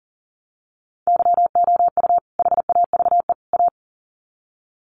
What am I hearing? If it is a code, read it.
Morse code, heard as a keyed tone, about 40 wpm, 700 Hz. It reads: YOU HAVE A
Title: Sets of 3 Words 40wpm